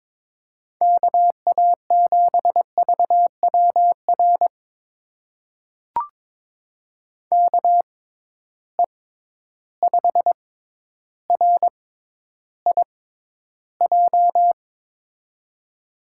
Morse code, heard as a keyed tone, about 22 wpm, spaced as for 11 wpm, 700 Hz. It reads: KA7VWR KE5RIJ
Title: USA Call Signs - 6 Characters - 6x Character Spacing - 22wpm